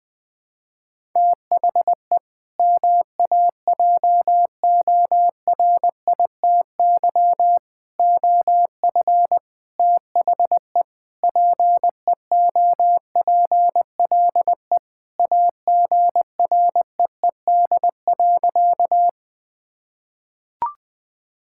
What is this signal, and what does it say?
Morse code, heard as a keyed tone, about 21 wpm, 700 Hz. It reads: THE MAJORITY OF THE PEOPLE AGREED.
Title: Sentences with 4-Word Phrases - 20wpm